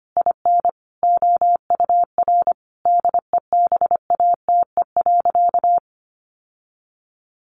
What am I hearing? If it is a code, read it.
Morse code, heard as a keyed tone, about 25 wpm, 700 Hz. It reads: IN OUR DEBATE.
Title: Sentences from Top 1500 Words 25wpm